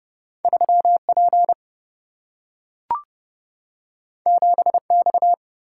Morse code, heard as a keyed tone, about 30 wpm, 700 Hz. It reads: 3P 7X
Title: Single-Letter Single-Number 30wpm